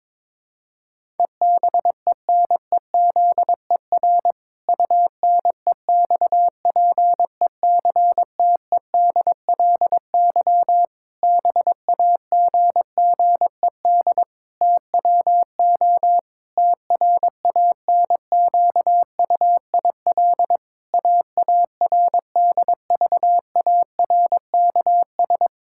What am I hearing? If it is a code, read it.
Morse code, heard as a keyed tone, about 22 wpm, 700 Hz. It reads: EBENEZER UNEXPECTEDLY BAGGED TWO TRANQUIL AARDVARKS
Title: Pangram Challenge v2 - 22wpm